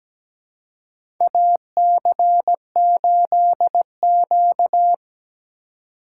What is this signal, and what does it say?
Morse code, heard as a keyed tone, about 17 wpm, 700 Hz. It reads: AC8Q
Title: USA Call Signs - 4 Characters - 3x Character Spacing - 17wpm